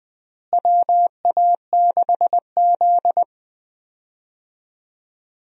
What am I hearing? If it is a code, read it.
Morse code, heard as a keyed tone, about 20 wpm, 700 Hz. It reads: WA6Z
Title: USA Call Signs - 4 Characters - Repeated 3x - 20wpm